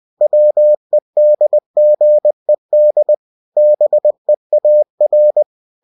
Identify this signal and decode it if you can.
Morse code, heard as a keyed tone, about 20 wpm, 600 Hz. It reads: WEDGED BEAR
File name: cw-books / winnie-the-pooh / winnie-20wpm-page-0019